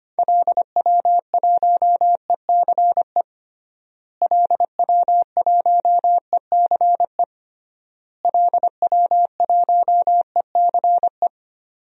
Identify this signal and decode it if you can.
Morse code, heard as a keyed tone, about 25 wpm, 700 Hz. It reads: LW1ECE LW1ECE LW1ECE
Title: DXCC-100 Call Signs - 6 Characters - Repeated 3x - 25wpm